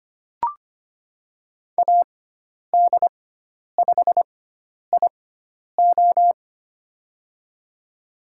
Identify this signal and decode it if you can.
Morse code, heard as a keyed tone, about 25 wpm, spaced as for 10 wpm, 700 Hz. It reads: AD5IO
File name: Lesson 16g - Letter c - Call Signs 25wpm